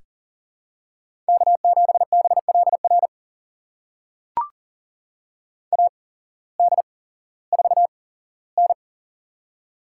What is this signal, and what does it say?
Morse code, heard as a keyed tone, about 40 wpm, spaced as for 19 wpm, 700 Hz. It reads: K7BLR AD4N
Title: USA Call Signs - 8x Character Spacing - 40wpm